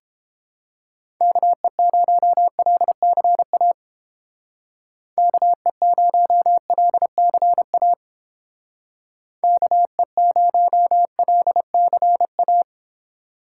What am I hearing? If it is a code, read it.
Morse code, heard as a keyed tone, about 29 wpm, 700 Hz. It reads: KE0LCA KE0LCA KE0LCA